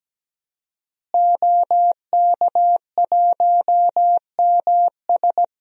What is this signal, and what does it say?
Morse code, heard as a keyed tone, about 17 wpm, 700 Hz. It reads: OK1MS
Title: DXCC-100 Call Signs - 5 Characters - Repeated 2x - 17wpm